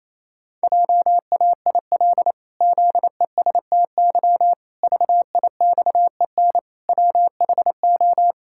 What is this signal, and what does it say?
Morse code, heard as a keyed tone, about 27 wpm, 700 Hz. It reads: JAIL ZESTY VIXEN WHO
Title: Pangram Challenge v2 - 28wpm